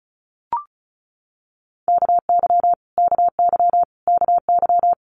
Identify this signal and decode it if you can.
Morse code, heard as a keyed tone, about 36 wpm, 700 Hz. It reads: KY KY KY
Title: Multi-letter Phonemes Repeated 3x 35wpm